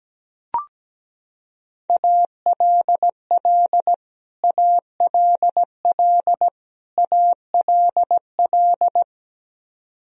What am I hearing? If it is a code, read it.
Morse code, heard as a keyed tone, about 17 wpm, 700 Hz. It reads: ALL ALL ALL
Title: Top 100 Words Repeated 17wpm